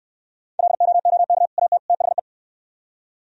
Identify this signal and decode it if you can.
Morse code, heard as a keyed tone, about 67 wpm, 700 Hz. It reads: LOOK AT THE